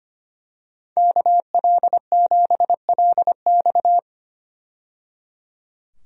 Morse code, heard as a keyed tone, about 25 wpm, 700 Hz. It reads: KL7LX
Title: DXCC-100 Call Signs - 5 Characters - 25wpm